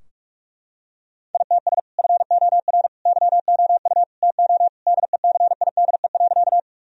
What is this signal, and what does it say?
Morse code, heard as a keyed tone, about 41 wpm, 700 Hz. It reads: ITS FOR YOU TO DECIDE.
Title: Sentences from Top 1000 Words - Review Part 3 - 45wpm